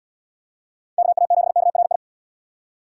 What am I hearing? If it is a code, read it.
Morse code, heard as a keyed tone, about 67 wpm, 700 Hz. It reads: KA7KRI